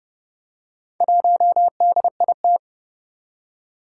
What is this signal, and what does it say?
Morse code, heard as a keyed tone, about 30 wpm, 700 Hz. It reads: 1DIT
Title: Four Challenging Letter-Number Combinations 30wpm